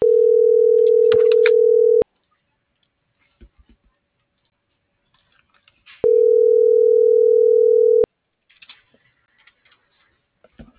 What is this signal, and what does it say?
Ringback tone (North America), 440 + 480 Hz.